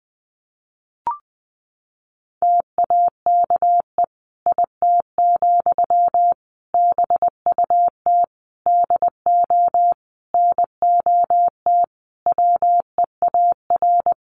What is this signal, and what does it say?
Morse code, heard as a keyed tone, about 20 wpm, 700 Hz. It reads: TAKE IT, BUT DO NOT WEAR
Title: Sentences from Top 1500 Words 20wpm